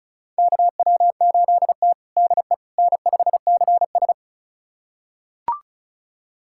Morse code, heard as a keyed tone, about 34 wpm, 700 Hz. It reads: KW8T DE N5CS
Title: USA Callsign DE Callsign - 4 Characters - 35wpm